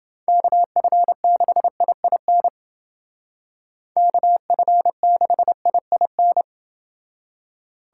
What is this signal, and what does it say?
Morse code, heard as a keyed tone, about 29 wpm, 700 Hz. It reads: KF6IIN KF6IIN